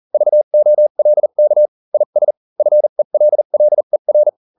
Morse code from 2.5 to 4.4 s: FELLER